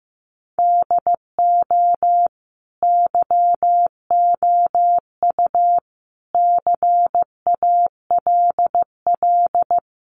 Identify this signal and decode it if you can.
Morse code, heard as a keyed tone, about 15 wpm, 700 Hz. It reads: DO YOU CALL
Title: Sentences from Top 1500 Words 15wpm